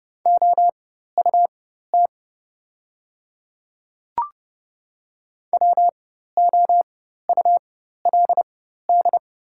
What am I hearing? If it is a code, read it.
Morse code, heard as a keyed tone, about 30 wpm, spaced as for 14 wpm, 700 Hz. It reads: OUT WOULD
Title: 4x Character Spacing - Top 100 Words - 30wpm